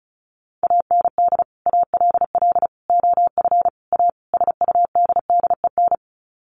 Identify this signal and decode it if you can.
Morse code, heard as a keyed tone, about 34 wpm, 700 Hz. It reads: AND ALL OF A SUDDEN
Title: Sets of 5 Words 35wpm